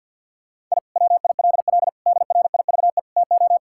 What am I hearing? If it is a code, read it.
Morse code, heard as a keyed tone, about 50 wpm, 700 Hz. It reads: I WILL DRIVE TO
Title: Sentences from Top 1000 Words - Review Part 6 - 50wpm